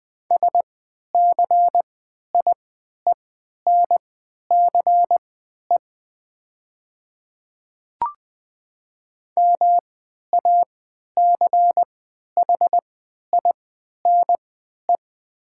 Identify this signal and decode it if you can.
Morse code, heard as a keyed tone, about 20 wpm, spaced as for 11 wpm, 700 Hz. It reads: SCIENCE MACHINE
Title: 3x Character Spacing - Top 500 Words - 20wpm